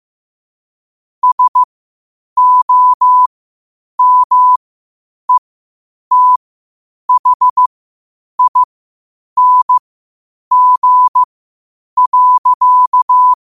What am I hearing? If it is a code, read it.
Morse code, heard as a keyed tone, about 15 wpm, spaced as for 8 wpm, 1000 Hz. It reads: SOMETHING.
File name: qotdfiles / 8wpm